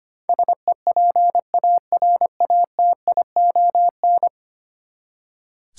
Morse code, heard as a keyed tone, about 25 wpm, 700 Hz. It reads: SEPARATION
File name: Ten Letter Words - 25wpm